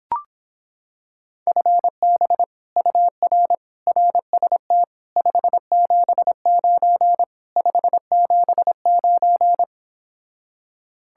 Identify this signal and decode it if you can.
Morse code, heard as a keyed tone, about 26 wpm, 700 Hz. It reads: FB UR RST 579 579